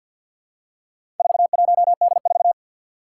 Morse code, heard as a keyed tone, about 53 wpm, 700 Hz. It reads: 41D4
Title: Four Challenging Letter-Number Combinations 50wpm